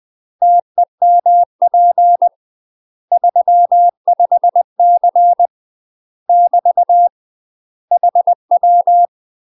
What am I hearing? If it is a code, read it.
Morse code, heard as a keyed tone, about 20 wpm, 700 Hz. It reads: TEMP 35C <BT> HW
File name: UK MASTER QSOs TRAINING / 20WPM / UK MASTER QSOs 0042